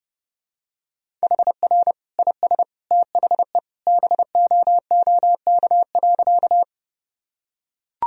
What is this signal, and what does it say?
Morse code, heard as a keyed tone, about 30 wpm, 700 Hz. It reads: HR IS THE BOOK.
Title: Sentences from Top 300 Words and Abbreviations 30wpm